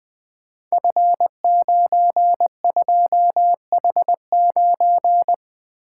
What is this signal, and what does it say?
Morse code, heard as a keyed tone, about 20 wpm, 700 Hz. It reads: F92H9
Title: Code Groups - 5 Letter:Numbers - 20wpm